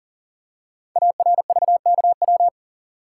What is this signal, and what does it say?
Morse code, heard as a keyed tone, about 39 wpm, 700 Hz. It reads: ARVKW
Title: Code Groups - 5 Letters - 40wpm